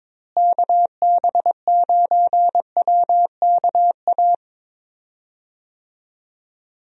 Morse code, heard as a keyed tone, about 22 wpm, 700 Hz. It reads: KB9WKA
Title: USA Call Signs - 6 Characters - Repeated 3x - 22wpm